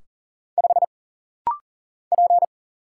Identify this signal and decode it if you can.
Morse code, heard as a keyed tone, about 40 wpm, 700 Hz. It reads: 5 P
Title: Single Letter-Number - Rapid-Fire 40wpm